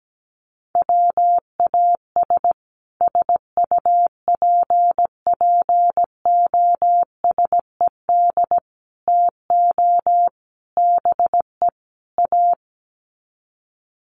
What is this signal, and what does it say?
Morse code, heard as a keyed tone, about 17 wpm, 700 Hz. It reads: WAS SUPPOSED TO BE A
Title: Sets of 5 Words Encore 17wpm